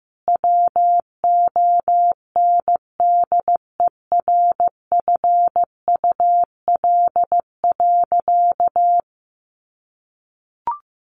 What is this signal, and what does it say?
Morse code, heard as a keyed tone, about 15 wpm, 700 Hz. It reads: WONDERFUL.
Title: Sentences from Top 1100 Words 15wpm